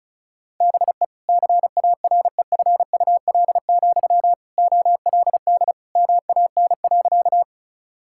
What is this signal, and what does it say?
Morse code, heard as a keyed tone, about 34 wpm, 700 Hz. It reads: BE CAREFUL, OLD MAN.
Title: Sentences from Top 1600 Words 35wpm